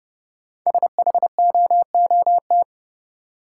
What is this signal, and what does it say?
Morse code, heard as a keyed tone, about 30 wpm, 700 Hz. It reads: SHOOT
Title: Sentences from Top 1200 Words - New Word Drill Alt - 30wpm